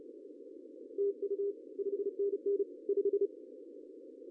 Morse code, heard as a keyed tone, about 32 wpm, 400 Hz. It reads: TU 5NN 5
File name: CW / cqww-cw-k5tr